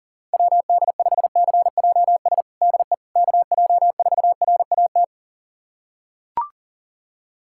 Morse code, heard as a keyed tone, about 40 wpm, 700 Hz. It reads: WD5CJS DE KJ4RAT
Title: USA Callsign DE Callsign 40wpm